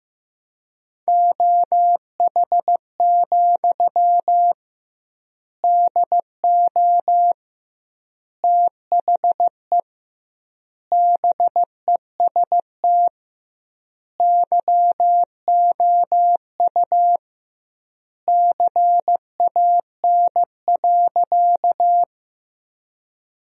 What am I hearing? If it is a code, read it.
Morse code, heard as a keyed tone, about 15 wpm, 700 Hz. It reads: OH, DO THE BEST YOU CAN.
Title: 2x Word-Spacing - Sentences from Top 500 Words - 15wpm